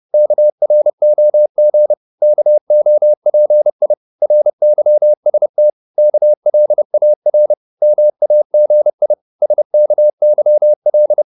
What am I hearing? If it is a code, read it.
Morse code, heard as a keyed tone, about 30 wpm, 600 Hz. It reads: KROG KOPI RYST KLAR MAGI SKYL
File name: Korte ord DK 30wpm